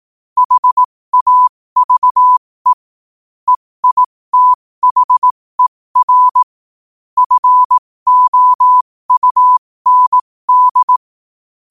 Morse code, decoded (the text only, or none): HAVE EITHER FOUND